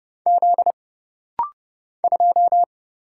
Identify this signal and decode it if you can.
Morse code, heard as a keyed tone, about 30 wpm, 700 Hz. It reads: Z 2
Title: Single Letter-Number - Rapid-Fire 30wpm